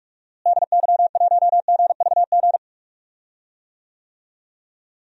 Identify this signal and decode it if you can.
Morse code, heard as a keyed tone, about 43 wpm, 700 Hz. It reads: DY1GVG
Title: DXCC-100 Call Signs - 6 Characters - Repeated 2x - 45wpm